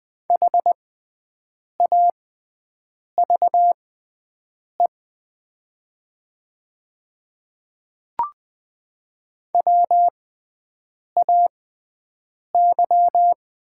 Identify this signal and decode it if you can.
Morse code, heard as a keyed tone, about 20 wpm, spaced as for 7 wpm, 700 Hz. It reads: HAVE WAY
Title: 6x Character Spacing - Top 100 Words - 20wpm